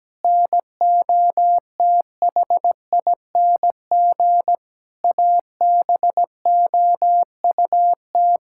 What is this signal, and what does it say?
Morse code, heard as a keyed tone, about 17 wpm, 700 Hz. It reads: NOTHING ABOUT
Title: Sentences from Top 1000 Words - Review Part 6 - 17wpm